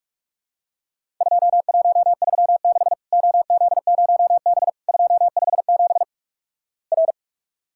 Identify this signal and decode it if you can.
Morse code, heard as a keyed tone, about 44 wpm, 700 Hz. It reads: 2136 OZ0B 257 R
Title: Worked All Europe DX Contest - QTC Exchange - 45wpm